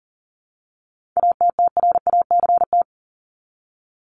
Morse code, heard as a keyed tone, about 40 wpm, 700 Hz. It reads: ATTRACT
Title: Top 1501-1750 Words 40wpm